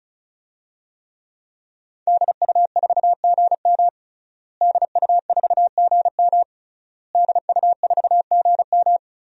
Morse code, heard as a keyed tone, about 36 wpm, 700 Hz. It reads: DU4GM DU4GM DU4GM
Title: DXCC-100 Call Signs - 5 Characters - Repeated 3x - 35wpm